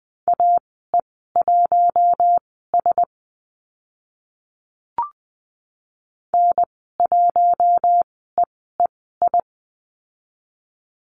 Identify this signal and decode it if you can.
Morse code, heard as a keyed tone, about 20 wpm, spaced as for 14 wpm, 700 Hz. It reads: AE1S N1EEI
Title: 042 Lesson 8g - Number 1 - Call Signs 20wpm